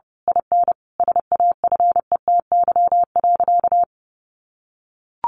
Morse code, heard as a keyed tone, about 28 wpm, 700 Hz. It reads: IN SAFETY.
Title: Sentences from Top 700 Words 30wpm